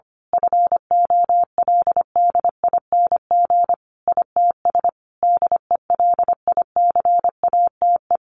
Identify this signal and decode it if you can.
Morse code, heard as a keyed tone, about 25 wpm, 700 Hz. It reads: FOLDING ITS DELICATE
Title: A Sound Of Thunder 25wpm